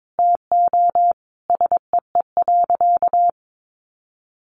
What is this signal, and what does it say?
Morse code, heard as a keyed tone, about 22 wpm, 700 Hz. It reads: TO SEE.
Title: Sentences from Top 1200 Words 22wpm